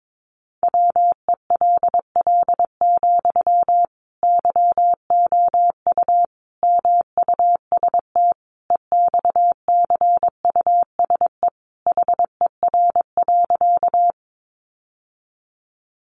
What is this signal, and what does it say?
Morse code, heard as a keyed tone, about 22 wpm, 700 Hz. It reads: WELL, YOU MUST EXCUSE HER.
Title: Sentences from Top 700 Words 22wpm